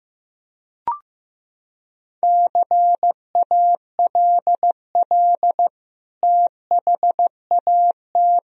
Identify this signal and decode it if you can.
Morse code, heard as a keyed tone, about 15 wpm, 700 Hz. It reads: CALL THAT
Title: Sentences from Top 1000 Words - Review Part 4 - 15wpm